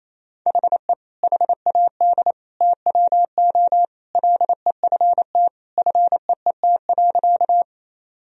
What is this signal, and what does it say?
Morse code, heard as a keyed tone, about 28 wpm, 700 Hz. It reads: HE HAD TWO LEFT FEET.
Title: Sentences from Top 1000 Words - Review Part 1 - 28wpm